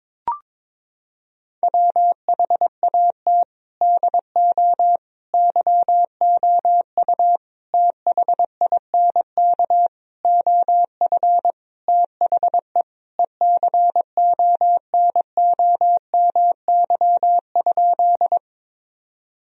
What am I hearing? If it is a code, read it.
Morse code, heard as a keyed tone, about 22 wpm, 700 Hz. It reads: WHAT DO YOU THINK OF THE ECONOMY?
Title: Sentences from Top 1000 Words - Review Part 3 - 22wpm